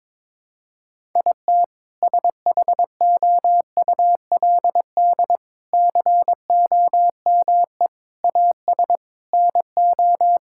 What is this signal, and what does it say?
Morse code, heard as a keyed tone, about 22 wpm, 700 Hz. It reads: IT SHOULD COME AS NO
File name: practice / 22wpm / Sets of 6 Words - 22wpm